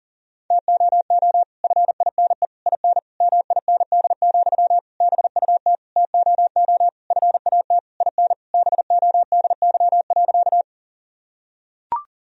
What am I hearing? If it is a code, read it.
Morse code, heard as a keyed tone, about 40 wpm, 700 Hz. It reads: TOO FINE IN MIND, BUT TOO FAT IN BODY.
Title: Sentences from Top 1000 Words - Review Part 4 - 40wpm